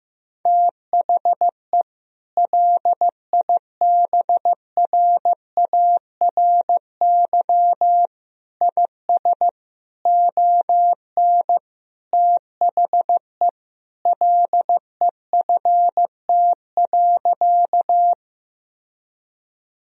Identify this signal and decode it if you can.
Morse code, heard as a keyed tone, about 15 wpm, 700 Hz. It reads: THE LIBRARY IS ON THE LEFT.
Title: Sentences from Top 1000 Words - Review Part 4 - 15wpm